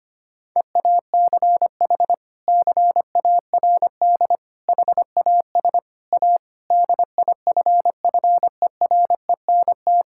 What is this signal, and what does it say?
Morse code, heard as a keyed tone, about 25 wpm, 700 Hz. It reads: EACH CARD HAS A DIFFERENT
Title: Sentences from Top 1000 Words - Review Part 6 - 25wpm